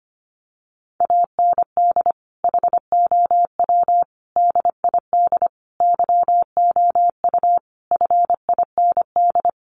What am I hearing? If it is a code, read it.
Morse code, heard as a keyed tone, about 25 wpm, 700 Hz. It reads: AND HOW DID YOU FIND
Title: Sentences from Top 100 Words 25wpm